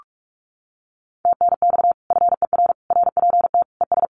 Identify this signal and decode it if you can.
Morse code, heard as a keyed tone, about 44 wpm, 700 Hz. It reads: TNX FER RPT ES
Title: Common QSO Phrases Encore 45wpm